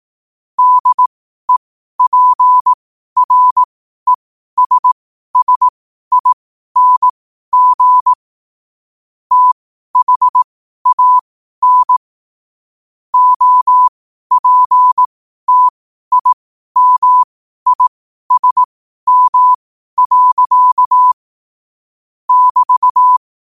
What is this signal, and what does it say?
Morse code, heard as a keyed tone, about 18 wpm, spaced as for 12 wpm, 1000 Hz. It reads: DEPRESSING THAN OPTIMISM. <BT>
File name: qotdfiles / 12wpm